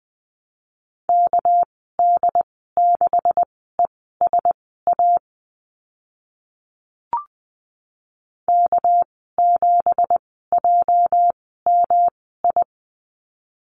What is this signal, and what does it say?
Morse code, heard as a keyed tone, about 20 wpm, spaced as for 14 wpm, 700 Hz. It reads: KD6ESA K7JMI
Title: 213 Lesson 34g - Number 8 - Call Signs 20wpm